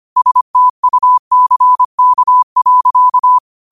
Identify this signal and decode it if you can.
Morse code, heard as a keyed tone, about 25 wpm, 1000 Hz. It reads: STUCK.
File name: qotdfiles / 25wpm